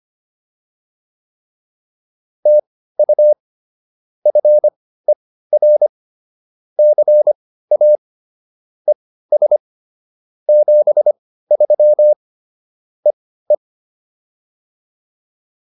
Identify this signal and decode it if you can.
Morse code, heard as a keyed tone, about 25 wpm, spaced as for 15 wpm, 600 Hz. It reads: TU FER CA ES 73 EE